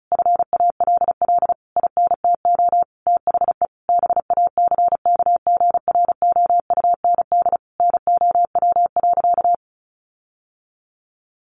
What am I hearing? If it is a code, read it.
Morse code, heard as a keyed tone, about 34 wpm, 700 Hz. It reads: FALL INTO THE BACKGROUND NOW.
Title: Sentences from Top 1300 Words 35wpm